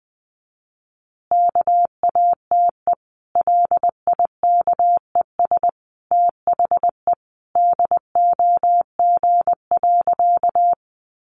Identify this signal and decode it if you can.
Morse code, heard as a keyed tone, about 20 wpm, 700 Hz. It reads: KATE LIKES THE DOG.
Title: Easy Phrases 20wpm